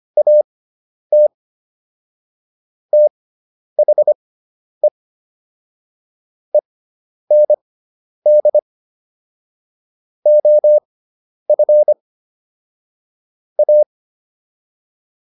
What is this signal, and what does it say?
Morse code, heard as a keyed tone, about 25 wpm, spaced as for 10 wpm, 600 Hz. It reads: AT THE END OF A
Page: intermediate-practice-files